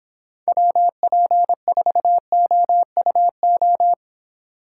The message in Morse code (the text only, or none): WP4OUO